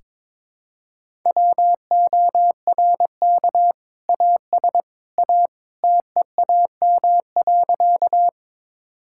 Morse code, heard as a keyed tone, about 21 wpm, 700 Hz. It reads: WORK AS A TEAM.
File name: practice / 22wpm / Sentences with 2-Word Phrases - 22wpm